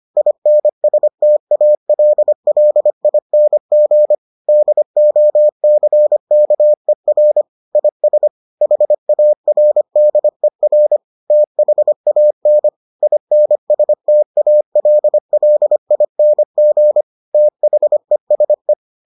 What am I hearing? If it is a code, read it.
Morse code, heard as a keyed tone, about 25 wpm, 600 Hz. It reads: INSTALLING DOCKER IS HARDER THAN INSTALLING THESE